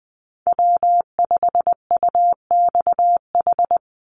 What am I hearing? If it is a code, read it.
Morse code, heard as a keyed tone, about 20 wpm, 700 Hz. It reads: W5UXH
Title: Call Signs Repeated 2x - 20wpm